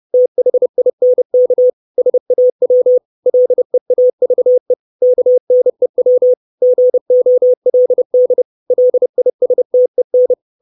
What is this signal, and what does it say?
Morse code, heard as a keyed tone, about 30 wpm, 500 Hz. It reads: THINK SAW LEAVE KNEW GOLD LISTEN
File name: 632 Words 30WPM